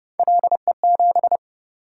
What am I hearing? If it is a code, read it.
Morse code, heard as a keyed tone, about 32 wpm, 700 Hz. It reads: LE7